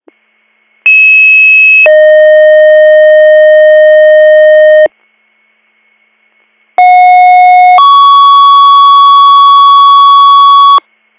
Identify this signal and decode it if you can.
Two-tone page, heard as a one-second tone, then a three-second tone, about 2600 then 620 Hz, about 730 then 1120 Hz. It is heard from 0.9 to 10.8 s.